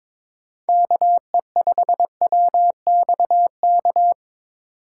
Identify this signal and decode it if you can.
Morse code, heard as a keyed tone, about 21 wpm, 700 Hz. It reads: KE5WXK
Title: USA Call Signs - 3x Character Spacing - 22wpm